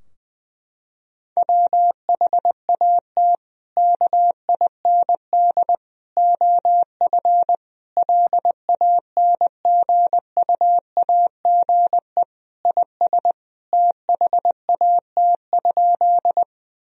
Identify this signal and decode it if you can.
Morse code, heard as a keyed tone, about 20 wpm, 700 Hz. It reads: WHAT KIND OF LANGUAGE IS THAT?
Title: Sentences from Top 1000 Words - Review Part 3 - 20wpm